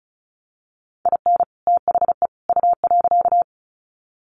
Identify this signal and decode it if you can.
Morse code, heard as a keyed tone, about 34 wpm, 700 Hz. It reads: IN THE U.
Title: Sets of 3 Words 35wpm